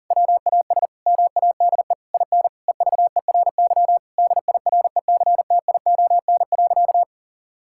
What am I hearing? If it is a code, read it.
Morse code, heard as a keyed tone, about 40 wpm, 700 Hz. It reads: WAS MADE IN EVERY DIRECTION.
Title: Sentences from Top 1000 Words - Review Part 5 - 40wpm